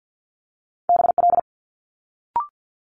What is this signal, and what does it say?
Morse code, heard as a keyed tone, about 50 wpm, 700 Hz. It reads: BL
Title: Multi-letter Phonemes Quick Fire 50wpm